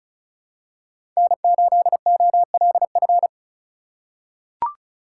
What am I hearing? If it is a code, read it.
Morse code, heard as a keyed tone, about 36 wpm, 700 Hz. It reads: N8OLF
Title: USA Call Signs - 4x Character Spacing - 35wpm